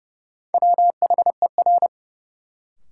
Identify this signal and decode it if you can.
Morse code, heard as a keyed tone, about 30 wpm, 700 Hz. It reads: WHER
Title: Sentences from Top 1500 Words - New Word Drill Alt - 30wpm